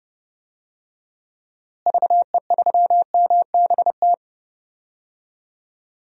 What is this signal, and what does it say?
Morse code, heard as a keyed tone, about 30 wpm, 700 Hz. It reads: VE3MBT
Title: DXCC-100 Call Signs - 6 Characters - Repeated 3x - 30wpm